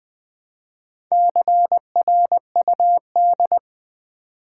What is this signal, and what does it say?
Morse code, heard as a keyed tone, about 20 wpm, 700 Hz. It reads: CRUD
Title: Build A Word - 1 Letter At A Time - 3 to 5-Letter Words - 20wpm